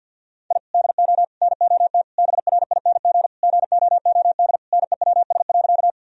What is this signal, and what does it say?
Morse code, heard as a keyed tone, about 50 wpm, 700 Hz. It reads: I DO NOT BRING GOOD NEWS.
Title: Sentences from Top 1000 Words - Review Part 3 - 50wpm